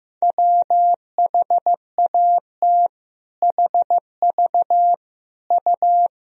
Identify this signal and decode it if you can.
Morse code, heard as a keyed tone, about 16 wpm, 700 Hz. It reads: WHAT HV U
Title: Sentences from Top 500 Words and Abbreviations 15wpm